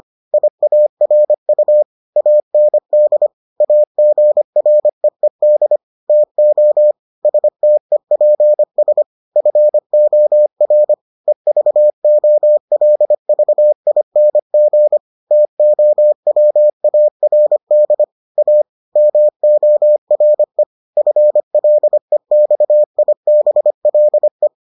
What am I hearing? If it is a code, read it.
Morse code, heard as a keyed tone, about 25 wpm, 600 Hz. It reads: IARU AND AGREED TO STEPS FOR EVOLVING TOWARD A MORE FLEXIBLE